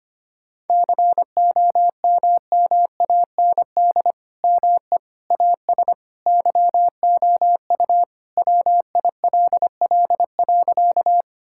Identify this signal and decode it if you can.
Morse code, heard as a keyed tone, about 25 wpm, 700 Hz. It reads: COMMAND ME AS YOU WILL.
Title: Sentences from Top 1600 Words 25wpm